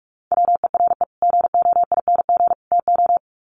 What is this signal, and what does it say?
Morse code, heard as a keyed tone, about 44 wpm, 700 Hz. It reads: WERE GOING TO
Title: Sets of 3 Words 45wpm